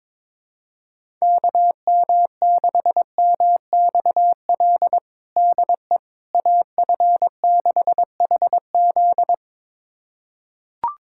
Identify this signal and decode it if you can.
Morse code, heard as a keyed tone, about 22 wpm, 700 Hz. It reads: KM6MXL DE AF6HZ
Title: USA Callsign DE Callsign 22wpm